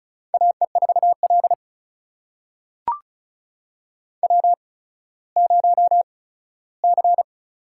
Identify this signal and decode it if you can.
Morse code, heard as a keyed tone, about 36 wpm, spaced as for 17 wpm, 700 Hz. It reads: AE4L W0C
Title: USA Call Signs - 4 Characters - 8x Character Spacing - 35wpm